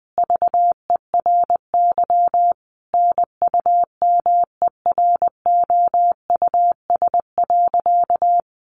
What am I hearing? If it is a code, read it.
Morse code, heard as a keyed tone, about 19 wpm, 700 Hz. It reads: VERY NUMEROUS.
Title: Sentences from Top 1400 Words 20wpm